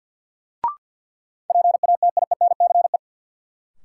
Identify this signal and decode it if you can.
Morse code, heard as a keyed tone, about 47 wpm, 700 Hz. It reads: PATIENCE
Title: Sentences from Top 1300 Words - New Word Drill - 50wpm